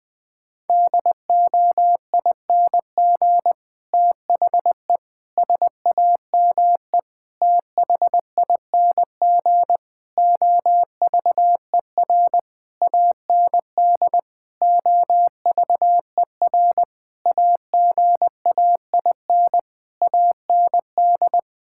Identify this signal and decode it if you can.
Morse code, heard as a keyed tone, about 20 wpm, 700 Hz. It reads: DOING THE SAME THING OVER AND OVER AGAIN AND
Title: Sentences with 8-Word Phrases - 20wpm